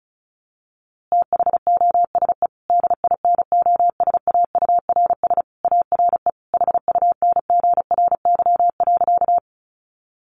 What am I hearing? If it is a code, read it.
Morse code, heard as a keyed tone, about 34 wpm, 700 Hz. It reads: THOSE DINOSAURS ARE HUNGRY.
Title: A Sound Of Thunder 35wpm